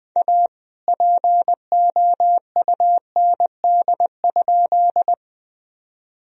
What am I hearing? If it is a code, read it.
Morse code, heard as a keyed tone, about 20 wpm, 700 Hz. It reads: A POUND?
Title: Sentences from Top 1000 Words - Review Part 2 - 20wpm